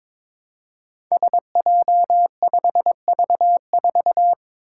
Morse code, heard as a keyed tone, about 22 wpm, 700 Hz. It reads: SJ5V4
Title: Five Challenging Letter-Number Combinations 22wpm